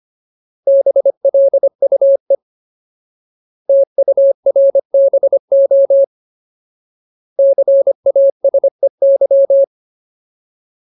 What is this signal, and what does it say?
Morse code, heard as a keyed tone, about 25 wpm, 550 Hz. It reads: BLUE TURBO CASEY
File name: Ultimate Word List / 25WPM Ultimate